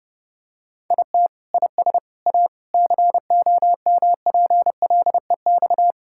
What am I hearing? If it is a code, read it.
Morse code, heard as a keyed tone, about 29 wpm, 700 Hz. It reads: IT IS A COMPLEX
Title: Sentences from Top 1000 Words - Review Part 6 - 30wpm